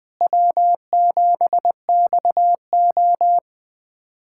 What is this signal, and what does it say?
Morse code, heard as a keyed tone, about 20 wpm, 700 Hz. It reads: W7XO
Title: USA Callsign DE Callsign - 4 Characters - 20wpm